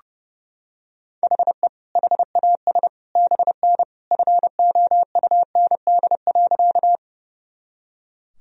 Morse code, heard as a keyed tone, about 31 wpm, 700 Hz. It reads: HE HAS BN FOUND.
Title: Sentences from Top 300 Words and Abbreviations 30wpm